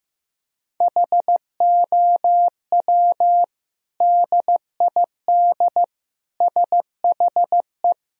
Morse code, heard as a keyed tone, about 15 wpm, 700 Hz. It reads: HOW DID SHE
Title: Sentences from Top 1000 Words - Review Part 3 - 15wpm